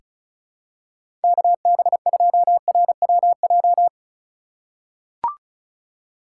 Morse code, heard as a keyed tone, about 36 wpm, 700 Hz. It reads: KB2RWJ
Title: USA Call Signs - 6 Characters - 3x Character Spacing - 35wpm